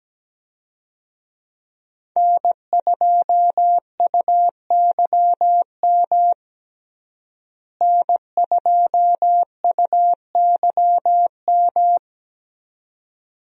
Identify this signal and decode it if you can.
Morse code, heard as a keyed tone, about 17 wpm, 700 Hz. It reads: N2UYM N2UYM
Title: USA Call Signs - 5 Characters - Repeated 2x - 17wpm